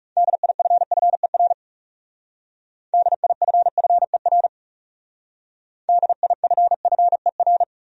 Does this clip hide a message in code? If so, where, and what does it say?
Yes: Morse code from 0.2 to 7.6 s, heard as a keyed tone, about 40 wpm, 700 Hz. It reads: DIFFER DIFFER DIFFER